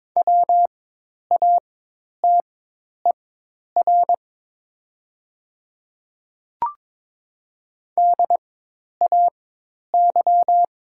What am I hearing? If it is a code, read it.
Morse code, heard as a keyed tone, about 22 wpm, spaced as for 10 wpm, 700 Hz. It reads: WATER DAY
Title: 4x Character Spacing - Top 100 Words - 22wpm